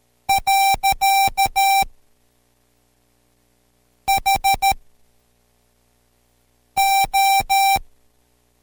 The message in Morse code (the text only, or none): . HO